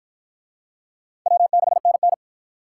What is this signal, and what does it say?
Morse code, heard as a keyed tone, about 50 wpm, 700 Hz. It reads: W6NN